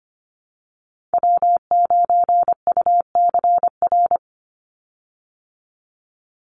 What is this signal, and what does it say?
Morse code, heard as a keyed tone, about 25 wpm, 700 Hz. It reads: W9UCR
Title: Call Signs Repeated 3x - 25wpm